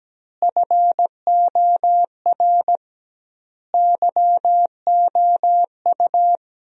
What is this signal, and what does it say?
Morse code, heard as a keyed tone, about 17 wpm, 700 Hz. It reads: FOR YOU
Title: 2x Word-Spacing - Sentences from Top 500 Words - 17wpm